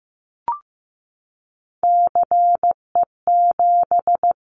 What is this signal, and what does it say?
Morse code, heard as a keyed tone, about 15 wpm, 700 Hz. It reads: CE7
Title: Callsign Prefixes for Top 100 DXCC - 15wpm